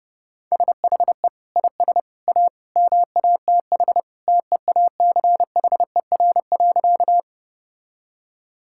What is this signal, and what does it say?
Morse code, heard as a keyed tone, about 32 wpm, 700 Hz. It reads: SHE IS A MATH TEACHER.
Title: Sentences from Top 1000 Words - Review Part 6 - 30wpm